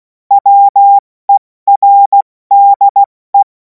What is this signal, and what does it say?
Morse code, heard as a keyed tone, about 16 wpm, 800 Hz. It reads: WERDE
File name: Morsekurs / Witze / Witz 03. Februar 2026 70 F80